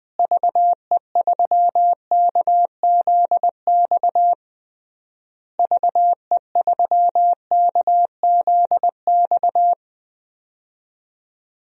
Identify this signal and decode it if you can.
Morse code, heard as a keyed tone, about 21 wpm, 700 Hz. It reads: VE3KZX VE3KZX
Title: DXCC-100 Call Signs - 6 Characters - Repeated 3x - 20wpm